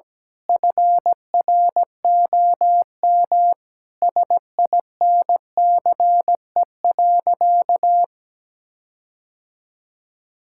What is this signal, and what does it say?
Morse code, heard as a keyed tone, about 17 wpm, 700 Hz. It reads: FROM SINCE.
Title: Sentences from Top 1600 Words 17wpm